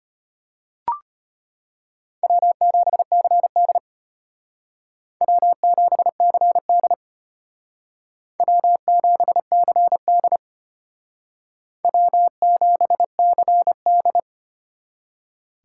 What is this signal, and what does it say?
Morse code, heard as a keyed tone, about 32 wpm, 700 Hz. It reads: W7CD W7CD W7CD W7CD